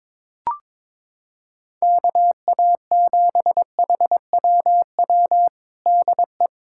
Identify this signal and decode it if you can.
Morse code, heard as a keyed tone, about 22 wpm, 700 Hz. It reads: KA7HWW DE
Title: USA Callsign DE Callsign 22wpm